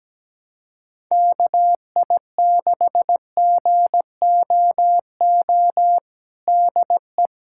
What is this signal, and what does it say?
Morse code, heard as a keyed tone, about 17 wpm, 700 Hz. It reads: KI6GOO DE
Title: USA Callsign DE Callsign - 6 Characters - 17wpm